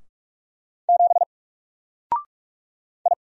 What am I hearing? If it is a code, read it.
Morse code, heard as a keyed tone, about 46 wpm, 700 Hz. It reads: 7 I
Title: Single Letter-Number 45wpm